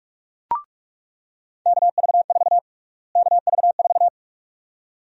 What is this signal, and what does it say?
Morse code, heard as a keyed tone, about 47 wpm, 700 Hz. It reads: KV4 KV4
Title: USA Call Sign Prefixes - Repeated 2x - 45wpm